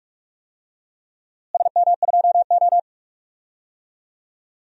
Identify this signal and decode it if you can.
Morse code, heard as a keyed tone, about 42 wpm, 700 Hz. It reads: SM2O
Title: DXCC-100 Call Signs - 4 Characters - Repeated 2x - 45wpm